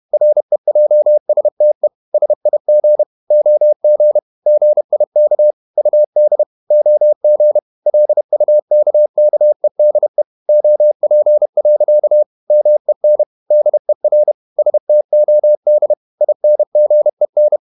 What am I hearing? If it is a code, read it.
Morse code, heard as a keyed tone, about 31 wpm, 600 Hz. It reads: REJSTE SIG OG GIK UD OG LUKKEDE OP. MEN DER STOD INGEN
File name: Ferie QRQ 31wpm